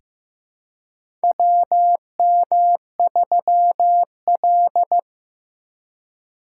Morse code, heard as a keyed tone, about 15 wpm, 700 Hz. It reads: WM3L
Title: USA Call Signs - 4 Characters - 4x Character Spacing - 15wpm